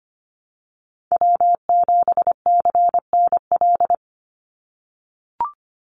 Morse code, heard as a keyed tone, about 25 wpm, 700 Hz. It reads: W7CNL
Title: NA SOTA Chaser Callsigns 25wpm